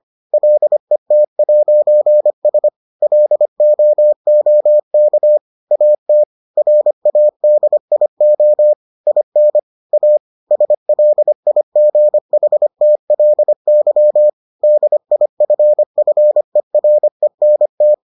Morse code, heard as a keyed tone, about 25 wpm, 600 Hz. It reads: LET'S LOOK AT RADIO IN A SLIGHTLY DIFFERENT